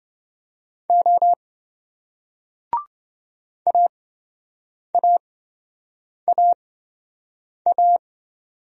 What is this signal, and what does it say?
Morse code, heard as a keyed tone, about 27 wpm, spaced as for 7 wpm, 700 Hz. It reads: O AAAA